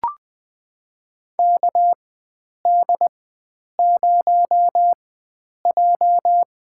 Morse code, heard as a keyed tone, about 21 wpm, spaced as for 9 wpm, 700 Hz. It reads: KD0J
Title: USA Call Signs - 4x Character Spacing - 20wpm